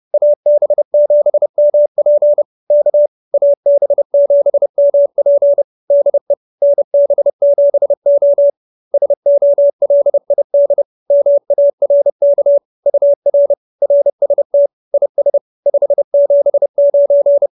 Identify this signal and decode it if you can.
Morse code, heard as a keyed tone, about 30 wpm, 570 Hz. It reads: AB7MP K AB7MP DE NB7O SOLID MARK UR RST IS 579